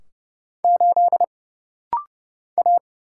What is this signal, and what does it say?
Morse code, heard as a keyed tone, about 30 wpm, 700 Hz. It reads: 8 A
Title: Single Letter-Number Alt - Rapid-Fire 30wpm